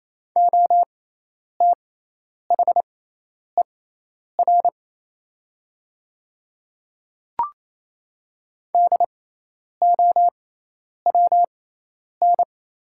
Morse code, heard as a keyed tone, about 27 wpm, spaced as for 10 wpm, 690 Hz. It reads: OTHER DOWN
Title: 6x Character Spacing - Top 100 Words - 28wpm